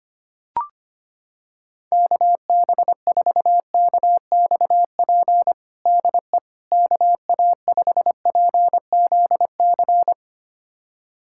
Morse code, heard as a keyed tone, about 25 wpm, 700 Hz. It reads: KB4KXP DE KA5PZC
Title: USA Callsign DE Callsign - 6 Characters - 25wpm